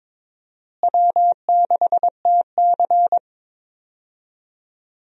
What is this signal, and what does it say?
Morse code, heard as a keyed tone, about 22 wpm, 700 Hz. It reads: W6TC
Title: USA Call Signs - 4 Characters - Repeated 2x - 22wpm